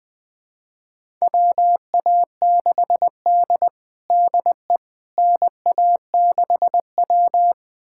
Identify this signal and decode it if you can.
Morse code, heard as a keyed tone, about 20 wpm, 700 Hz. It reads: WA6D DE NA6W
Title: USA Callsign DE Callsign - 4 Characters - 20wpm